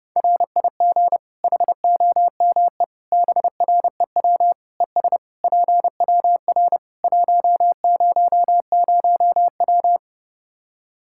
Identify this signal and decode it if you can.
Morse code, heard as a keyed tone, about 30 wpm, 700 Hz. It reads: RIG HOME BREW ES PWR 100W
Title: Practice QSO - Exchange 2 - Info - 30wpm